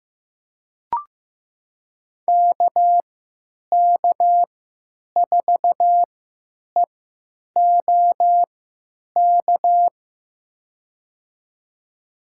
Morse code, heard as a keyed tone, about 16 wpm, spaced as for 9 wpm, 700 Hz. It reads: KK4EOK
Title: USA Call Signs - 3x Character Spacing - 15wpm